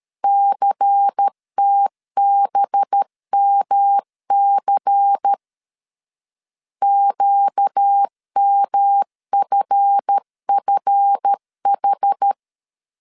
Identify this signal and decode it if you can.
Morse code, heard as a keyed tone, about 13 wpm, 800 Hz. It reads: CTBMC QMFFH
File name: LEZIONI / 7 GF9Z / 7 GF 12wpm 2122